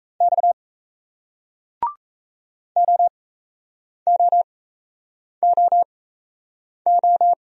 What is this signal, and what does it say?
Morse code, heard as a keyed tone, about 38 wpm, spaced as for 8 wpm, 700 Hz. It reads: X OOOO